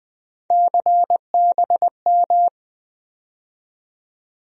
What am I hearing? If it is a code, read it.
Morse code, heard as a keyed tone, about 20 wpm, 700 Hz. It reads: CBM
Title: USA Call Sign Postfixes - 3 Letters 20wpm